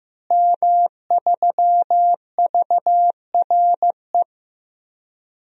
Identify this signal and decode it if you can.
Morse code, heard as a keyed tone, about 15 wpm, 700 Hz. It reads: M3VRE
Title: DXCC-100 Call Signs - 5 Characters - Repeated 3x - 15wpm